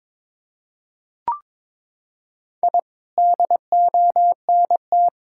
Morse code, heard as a keyed tone, about 22 wpm, 700 Hz. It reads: I DONT
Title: Sentences from Top 1000 Words - Review Part 5 - 22wpm